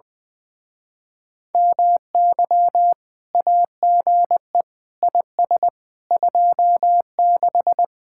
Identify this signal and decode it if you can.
Morse code, heard as a keyed tone, about 21 wpm, 700 Hz. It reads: MY AGE IS 26
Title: Speed-Racing Common QSO Phrases - 20wpm